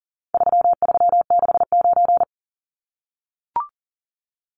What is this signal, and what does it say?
Morse code, heard as a keyed tone, about 40 wpm, 700 Hz. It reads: 3369
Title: Numbers - 4-Digit 40wpm